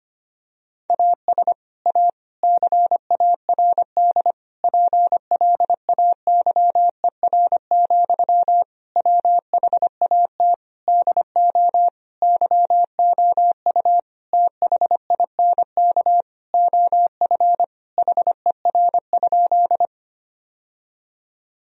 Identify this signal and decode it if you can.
Morse code, heard as a keyed tone, about 25 wpm, 700 Hz. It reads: AS A CARD PLAYER, WHAT DO YOU THINK OF HER?
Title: Sentences from Top 1000 Words - Review Part 4 - 25wpm